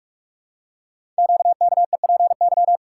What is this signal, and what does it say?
Morse code, heard as a keyed tone, about 44 wpm, 700 Hz. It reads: QKEPY
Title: Code Groups - 5 Letters - 45wpm